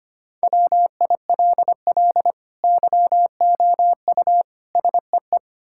Morse code, heard as a keyed tone, about 25 wpm, 700 Hz. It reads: WILL YOU SEE
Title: Sentences from Top 1600 Words 25wpm